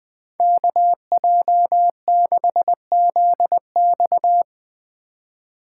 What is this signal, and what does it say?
Morse code, heard as a keyed tone, about 23 wpm, 700 Hz. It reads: KJ6ZX
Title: Speed Racing - USA 2x2 Call Signs 20wpm